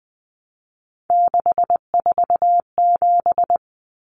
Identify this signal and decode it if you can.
Morse code, heard as a keyed tone, about 20 wpm, 700 Hz. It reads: 647
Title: Numbers - 3-Digit 20wpm